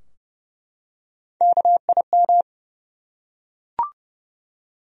Morse code, heard as a keyed tone, about 30 wpm, 700 Hz. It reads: KIM
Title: USA Common Names - Max 3 Letters - 30wpm